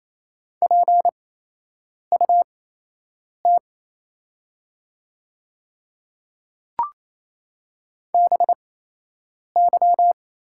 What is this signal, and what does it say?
Morse code, heard as a keyed tone, about 28 wpm, spaced as for 8 wpm, 690 Hz. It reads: PUT BY
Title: 8x Character Spacing - Top 200 Words - 28wpm